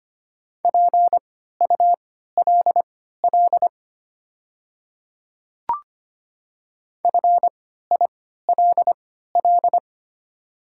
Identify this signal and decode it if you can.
Morse code, heard as a keyed tone, about 25 wpm, spaced as for 14 wpm, 700 Hz. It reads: PULL FILL
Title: 3x Character Spacing - Top 500 Words - 25wpm